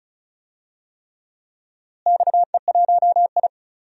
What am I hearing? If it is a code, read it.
Morse code, heard as a keyed tone, about 36 wpm, 700 Hz. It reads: XE1I
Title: DXCC-100 Call Signs - 4 Characters - Repeated 3x - 35wpm